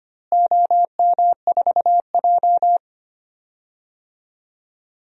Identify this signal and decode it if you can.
Morse code, heard as a keyed tone, about 25 wpm, 700 Hz. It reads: OM4J
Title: DXCC-100 Call Signs - 4 Characters - Repeated 3x - 25wpm